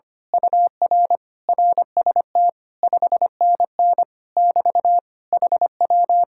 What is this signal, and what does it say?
Morse code, heard as a keyed tone, about 24 wpm, 700 Hz. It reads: UR RST 5NN <BT> HW
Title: Speed-Racing Common QSO Phrases - 25wpm